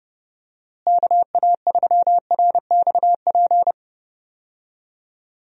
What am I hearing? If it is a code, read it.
Morse code, heard as a keyed tone, about 30 wpm, 700 Hz. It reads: KA3RXP
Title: USA Call Signs - 6 Characters - Repeated 3x - 30wpm